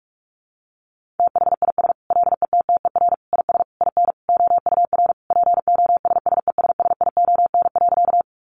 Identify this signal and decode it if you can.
Morse code, heard as a keyed tone, about 44 wpm, 700 Hz. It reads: THIS LETTER IS IN OUR POSSESSION.
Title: Sentences from Top 1400 Words 45wpm